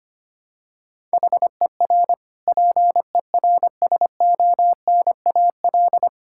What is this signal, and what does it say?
Morse code, heard as a keyed tone, about 25 wpm, 700 Hz. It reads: HER PERSONAL
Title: Sentences from Top 1000 Words - Review Part 6 - 25wpm